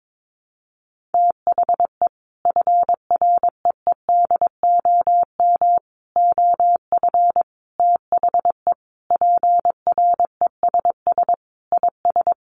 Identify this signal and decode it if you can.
Morse code, heard as a keyed tone, about 22 wpm, 700 Hz. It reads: THE FREEDOM OF THE PRESS IS
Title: Sentences from Top 800 Words 22wpm